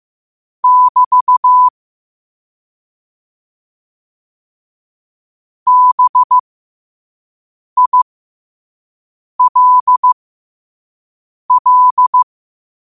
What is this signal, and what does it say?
Morse code, heard as a keyed tone, about 15 wpm, spaced as for 5 wpm, 1000 Hz. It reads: <BT> BILL
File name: qotdfiles / 5wpm